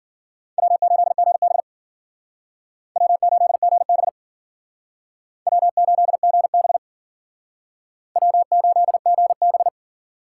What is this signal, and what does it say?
Morse code, heard as a keyed tone, about 53 wpm, 700 Hz. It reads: W8GB W8GB W8GB W8GB